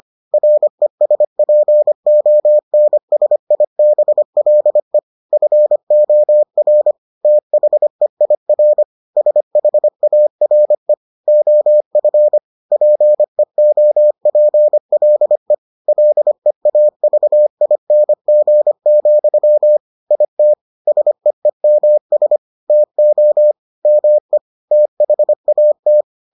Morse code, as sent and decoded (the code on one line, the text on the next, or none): .-. ... .--. --- -. ... .. -... .-.. / ..-. --- .-. / - .... .. .-. / ... .... .- .-. / --- ..-. / .--. --- .--. .-.. / .-.. .- ...- .. -. --. --..-- / .. - / ... . -- ... / - --- / -- . / - .... .- -
RESPONSIBLE FOR THEIR SHARE OF PEOPLE LEAVING, IT SEEMS TO ME THAT